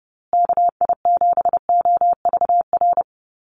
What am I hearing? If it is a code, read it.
Morse code, heard as a keyed tone, about 30 wpm, 700 Hz. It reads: KI7OVR
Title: USA Call Signs Repeated 3x 30wpm